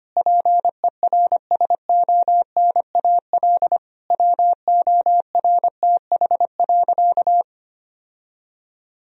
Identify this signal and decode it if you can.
Morse code, heard as a keyed tone, about 25 wpm, 700 Hz. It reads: PERSONAL WORTH.
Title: Sentences from Top 1000 Words - Review Part 4 - 25wpm